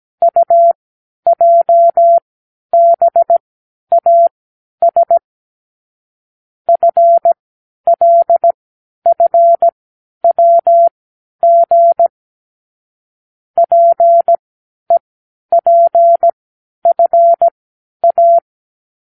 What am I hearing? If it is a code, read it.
Morse code, heard as a keyed tone, about 17 wpm, spaced as for 10 wpm, 680 Hz. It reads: UJBAS FLFWG PEPFA
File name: k7qo / Lesson 36 (A-X)